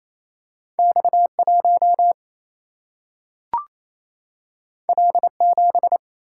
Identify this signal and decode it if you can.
Morse code, heard as a keyed tone, about 28 wpm, 700 Hz. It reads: X1 L7
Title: Single-Letter Single-Number 28wpm